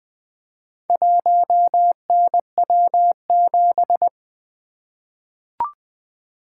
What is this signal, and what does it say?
Morse code, heard as a keyed tone, about 20 wpm, 700 Hz. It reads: 1NW7
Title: Four Letter-Number Combinations - 20wpm